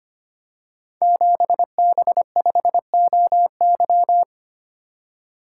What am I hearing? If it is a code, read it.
Morse code, heard as a keyed tone, about 25 wpm, 700 Hz. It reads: 7B5OY
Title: Code Groups - 5 Letter:Numbers - 25wpm